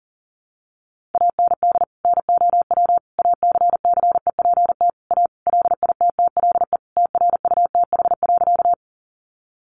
Morse code, heard as a keyed tone, about 40 wpm, 700 Hz. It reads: AND NOW ACCEPT A LITTLE TRUTH.
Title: Sentences from Top 1300 Words 40wpm